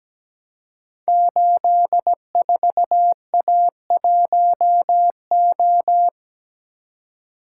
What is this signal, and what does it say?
Morse code, heard as a keyed tone, about 17 wpm, 700 Hz. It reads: 84A1O
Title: Code Groups - 5 Letter:Numbers - 17wpm